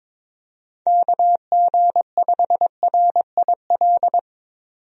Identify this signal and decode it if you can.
Morse code, heard as a keyed tone, about 22 wpm, 700 Hz. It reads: KG5RIL
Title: USA Call Signs - 6 Characters - 22wpm